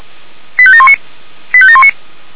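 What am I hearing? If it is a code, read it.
Five-tone selective call (ZVEI), heard as quick runs of five tones, tones of about 70 ms. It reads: 85719 85719
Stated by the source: ZVEI five-tone selective call received on an SDR.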